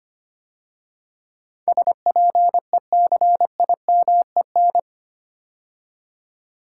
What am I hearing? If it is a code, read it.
Morse code, heard as a keyed tone, about 25 wpm, 700 Hz. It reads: SPECIMEN
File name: 8 Letter Words - First 3 Letters Given - 25wpm